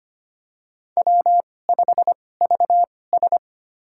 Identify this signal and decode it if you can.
Morse code, heard as a keyed tone, about 25 wpm, spaced as for 18 wpm, 700 Hz. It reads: W5VS
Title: USA Call Signs - 4 Characters - 2x Character Spacing - 25wpm